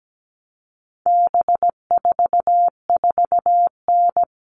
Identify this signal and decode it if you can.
Morse code, heard as a keyed tone, about 17 wpm, 700 Hz. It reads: B44N
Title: All Cut Numbers - 4-Digit 17wpm